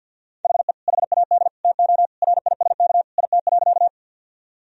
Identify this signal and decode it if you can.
Morse code, heard as a keyed tone, about 50 wpm, 700 Hz. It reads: HE HAD TO RISK IT.
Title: Sentences from Top 1000 Words - Review Part 4 - 50wpm